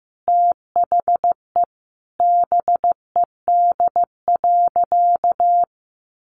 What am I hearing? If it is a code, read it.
Morse code, heard as a keyed tone, about 14 wpm, 700 Hz. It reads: THE BED.
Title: Sentences from Top 1400 Words 15wpm